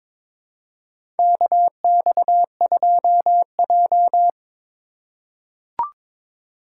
Morse code, heard as a keyed tone, about 23 wpm, 700 Hz. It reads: KX2J
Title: USA Call Signs - 4 Characters - 2x Character Spacing - 22wpm